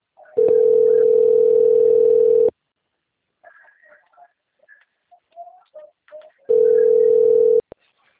Ringback tone (North America), 440 + 480 Hz.